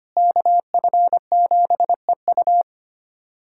Morse code, heard as a keyed tone, about 25 wpm, 700 Hz. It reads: KF7EU
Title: USA Callsign DE Callsign - 5 Characters - 25wpm